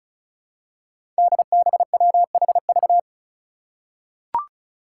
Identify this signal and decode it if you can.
Morse code, heard as a keyed tone, about 34 wpm, 700 Hz. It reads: DBWHV
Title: Five Challenging Letter-Number Combinations 35wpm